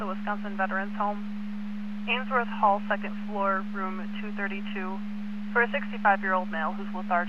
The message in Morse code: none

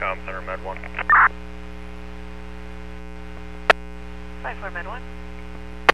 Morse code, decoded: none

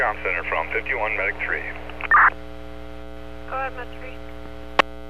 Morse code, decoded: none

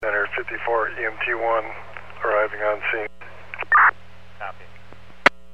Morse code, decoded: none

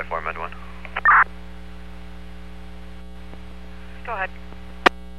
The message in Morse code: none

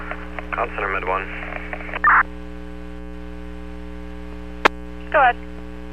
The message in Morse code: none

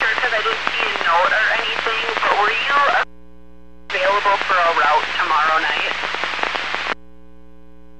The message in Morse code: none

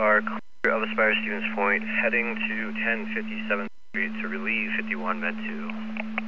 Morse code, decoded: none

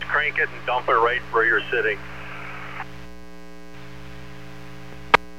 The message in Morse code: none

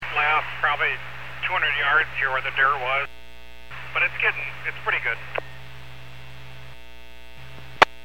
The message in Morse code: none

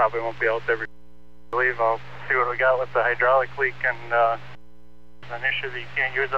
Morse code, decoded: none

